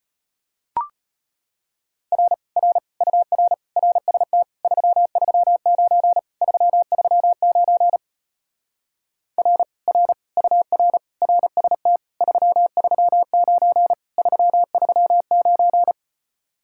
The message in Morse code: R R UR RST 339 339 R R UR RST 339 339

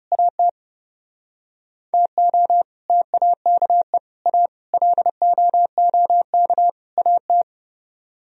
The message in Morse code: AT TO TAKE A LOOK AT

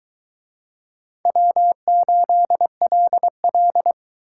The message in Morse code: W8LL

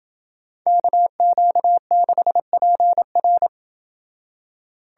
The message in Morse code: KQ6PR

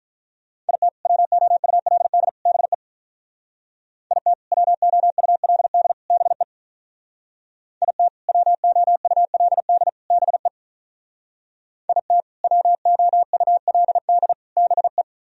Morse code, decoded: IT WOULD BE IT WOULD BE IT WOULD BE IT WOULD BE

